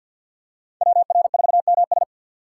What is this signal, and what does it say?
Morse code, heard as a keyed tone, about 52 wpm, 700 Hz. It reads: WR4MS
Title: USA Call Signs - 5 Characters - 3x Character Spacing - 50wpm